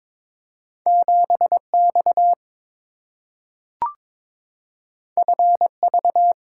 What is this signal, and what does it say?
Morse code, heard as a keyed tone, about 22 wpm, 700 Hz. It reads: 7X FV
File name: practice / 22wpm / Two Letter-Number Combinations - 22wpm